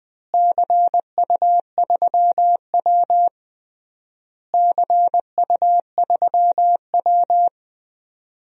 Morse code, heard as a keyed tone, about 19 wpm, 700 Hz. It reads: CU3W CU3W
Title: DXCC-100 Call Signs - 4 Characters - Repeated 3x - 20wpm